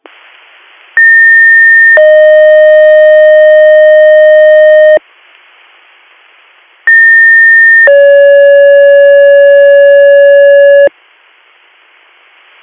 Two-tone page, heard as a one-second tone, then a three-second tone, about 1800 then 620 Hz, about 1800 then 560 Hz. It is heard from 1.0 to 10.9 s.
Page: bradford-pa